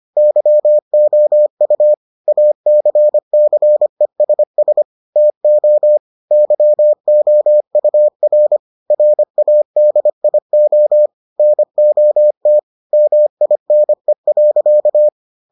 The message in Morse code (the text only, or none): YOU ACCESS TO YOUR RADIO NOT MINE.